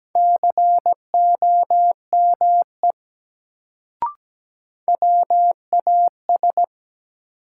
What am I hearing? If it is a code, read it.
Morse code, heard as a keyed tone, about 17 wpm, 700 Hz. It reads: COME WAS
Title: Sentences from Top 100 Words - New Word Drill - 17wpm